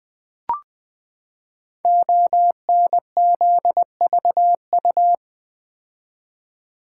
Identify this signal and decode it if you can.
Morse code, heard as a keyed tone, about 20 wpm, 700 Hz. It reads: ONZVU
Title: Code Groups - 5 Letter:Numbers - 20wpm